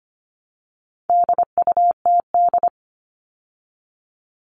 Morse code, heard as a keyed tone, about 25 wpm, 700 Hz. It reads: DUTD
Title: All Cut Numbers - 4-Digit 25wpm